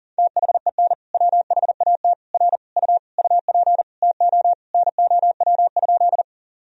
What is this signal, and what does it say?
Morse code, heard as a keyed tone, about 40 wpm, 700 Hz. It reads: THEN WHAT R U UP TO NOW?
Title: Sentences from Top 100 Words and Abbreviations 40wpm